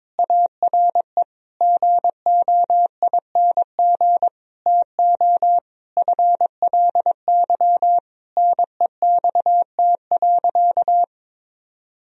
Morse code, heard as a keyed tone, about 22 wpm, 700 Hz. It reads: ARE GOING TO FLY NEXT.
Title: Sentences from Top 1000 Words - Review Part 2 - 22wpm